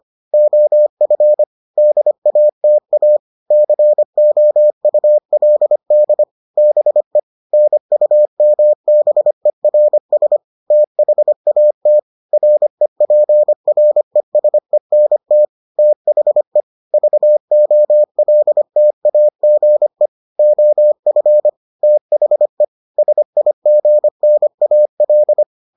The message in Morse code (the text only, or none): OF DATA COULD BE NUMBERS THAT REPRESENT THE VOLTAGE OF THE SIGNAL